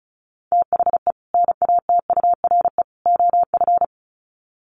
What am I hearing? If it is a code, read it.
Morse code, heard as a keyed tone, about 34 wpm, 700 Hz. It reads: THE NATURE OF
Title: Sets of 3 Words 35wpm